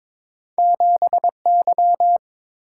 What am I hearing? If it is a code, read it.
Morse code, heard as a keyed tone, about 23 wpm, 700 Hz. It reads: 7Y
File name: practice / 22wpm / Two Letter-Number Combinations - 22wpm